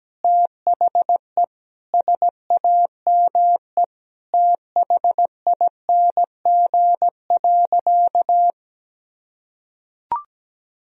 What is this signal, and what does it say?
Morse code, heard as a keyed tone, about 17 wpm, 700 Hz. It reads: THE SAME THING.
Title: Sentences from Top 1000 Words - Review Part 1 - 17wpm